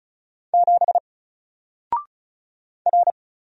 Morse code, heard as a keyed tone, about 34 wpm, 700 Hz. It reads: 7 R
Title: Single Letter-Number Alt 35wpm